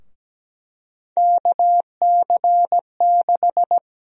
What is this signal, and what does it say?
Morse code, heard as a keyed tone, about 17 wpm, 700 Hz. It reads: KC6
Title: USA Call Sign Prefixes - 17wpm